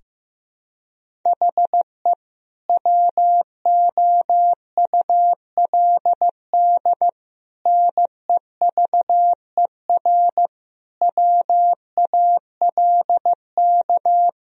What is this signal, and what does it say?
Morse code, heard as a keyed tone, about 16 wpm, 700 Hz. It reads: HE WOULD NEVER WALK
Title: Sentences from Top 1000 Words - Review Part 1 - 15wpm